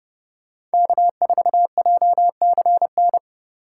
Morse code, heard as a keyed tone, about 29 wpm, 700 Hz. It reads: K4JCN
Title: USA Call Signs - 5 Characters - 4x Character Spacing - 30wpm